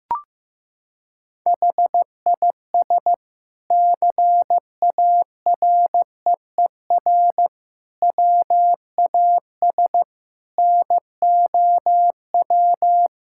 Morse code, heard as a keyed tone, about 15 wpm, 700 Hz. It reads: HIS CAREER WAS NOW
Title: Sentences from Top 1000 Words - Review Part 6 - 15wpm